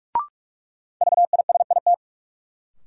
Morse code, heard as a keyed tone, about 47 wpm, 700 Hz. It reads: VISIT
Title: Sentences from Top 900 Words - New Word Drill - 45wpm